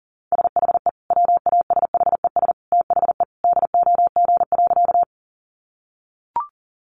Morse code, heard as a keyed tone, about 40 wpm, 700 Hz. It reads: SHE WASHES THE DOG.
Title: Easy Phrases 40wpm